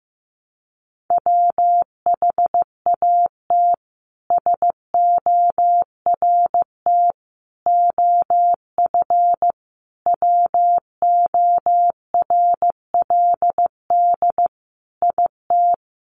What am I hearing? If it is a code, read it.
Morse code, heard as a keyed tone, about 15 wpm, 700 Hz. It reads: WHAT SORT OF WORLD IT
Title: A Sound Of Thunder 15wpm